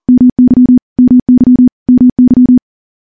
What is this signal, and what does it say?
Morse code, heard as a keyed tone, about 39 wpm, 250 Hz. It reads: MY MY MY
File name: morse / morse-training-40WPM